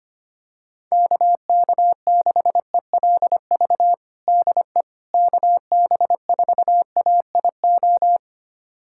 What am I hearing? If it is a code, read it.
Morse code, heard as a keyed tone, about 25 wpm, 700 Hz. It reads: KK6ELV DE KB4AIO
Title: USA Callsign DE Callsign - 6 Characters - 25wpm